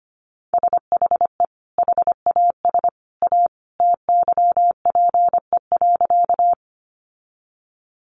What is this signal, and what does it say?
Morse code, heard as a keyed tone, about 25 wpm, 700 Hz. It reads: SHE HAS A TYPE.
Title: Sentences from Top 600 Words 25wpm